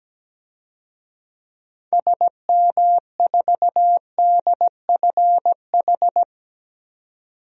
Morse code, heard as a keyed tone, about 17 wpm, 700 Hz. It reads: SM4DFH
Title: DXCC-100 Call Signs - 6 Characters - 17wpm